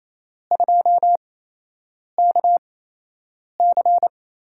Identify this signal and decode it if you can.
Morse code, heard as a keyed tone, about 28 wpm, spaced as for 8 wpm, 700 Hz. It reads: 2KC
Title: USA Call Signs - 5 Characters - 8x Character Spacing - 28wpm